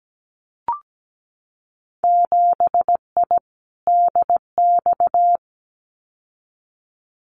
Morse code, heard as a keyed tone, about 17 wpm, 700 Hz. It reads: 7I DX
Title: Winter Field Day Exchanges Encore 17wpm